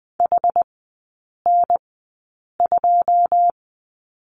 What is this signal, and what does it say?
Morse code, heard as a keyed tone, about 20 wpm, spaced as for 8 wpm, 690 Hz. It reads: HN2
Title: 081 Lesson 14e - Number 2 - Three Characters 20wpm